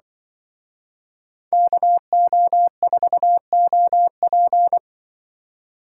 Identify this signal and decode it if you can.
Morse code, heard as a keyed tone, about 23 wpm, 700 Hz. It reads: KO4OP
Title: Speed Racing - USA 2x2 Call Signs 20wpm